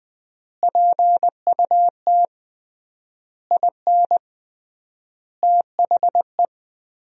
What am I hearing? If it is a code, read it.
Morse code, heard as a keyed tone, about 21 wpm, 700 Hz. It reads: PUT IN THE
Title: 3x Word-Spacing - Sentences from Top 200 Words - 20wpm